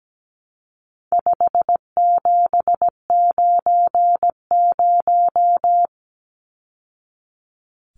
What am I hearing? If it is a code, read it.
Morse code, heard as a keyed tone, about 17 wpm, 700 Hz. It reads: 5790
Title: Numbers - 4-Digit 17wpm